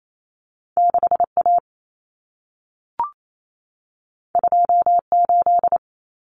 Morse code, heard as a keyed tone, about 28 wpm, 700 Hz. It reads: 6A 28
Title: Common Cut Numbers - 2-Digit 28wpm